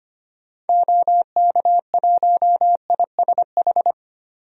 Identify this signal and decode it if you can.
Morse code, heard as a keyed tone, about 25 wpm, 700 Hz. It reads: OK1ISH
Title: DXCC-100 Call Signs - 6 Characters - 25wpm